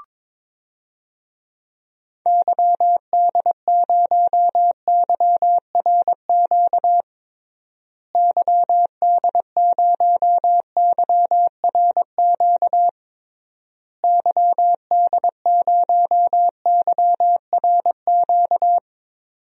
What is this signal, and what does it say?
Morse code, heard as a keyed tone, about 22 wpm, 700 Hz. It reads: YD0YRQ YD0YRQ YD0YRQ
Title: DXCC-100 Call Signs - 6 Characters - Repeated 3x - 22wpm